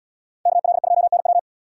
Morse code, heard as a keyed tone, about 78 wpm, 700 Hz. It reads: KL2NW